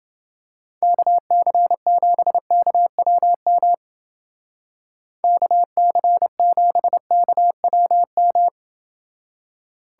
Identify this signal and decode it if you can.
Morse code, heard as a keyed tone, about 29 wpm, 700 Hz. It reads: KC7KWM KC7KWM